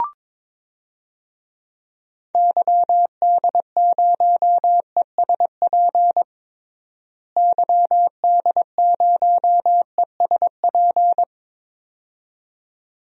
Morse code, heard as a keyed tone, about 22 wpm, 700 Hz. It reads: YD0ESP YD0ESP
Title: DXCC-100 Call Signs - 6 Characters - Repeated 2x - 22wpm